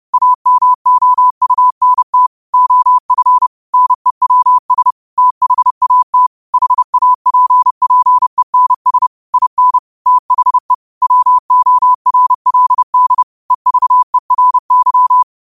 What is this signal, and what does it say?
Morse code, heard as a keyed tone, about 30 wpm, 1000 Hz. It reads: AMOUNT OF NEWS THAT HAPPENS IN THE WORLD EVERY
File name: 30wpm